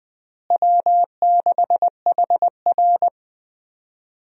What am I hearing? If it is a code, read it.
Morse code, heard as a keyed tone, about 20 wpm, 690 Hz. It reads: W6HR
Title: USA Call Signs - 4 Characters - Repeated 3x - 20wpm